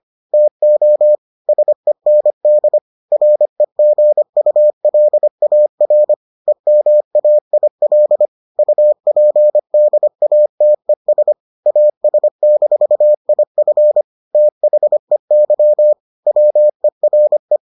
Morse code, heard as a keyed tone, about 25 wpm, 600 Hz. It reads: TO SEND REGULAR EMAIL UPDATES AS-IF THEY WERE